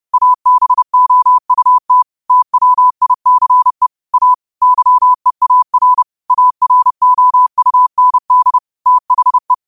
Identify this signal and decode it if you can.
Morse code, heard as a keyed tone, about 30 wpm, 1000 Hz. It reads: ABOUT TWICE A YEAR AROUND THE
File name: qotdfiles / 30wpm